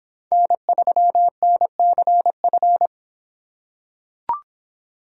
Morse code, heard as a keyed tone, about 24 wpm, 700 Hz. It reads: N3NCF